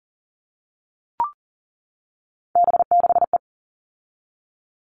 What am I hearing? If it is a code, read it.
Morse code, heard as a keyed tone, about 40 wpm, 700 Hz. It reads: B6E
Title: All Cut Numbers - 3-Digit 40wpm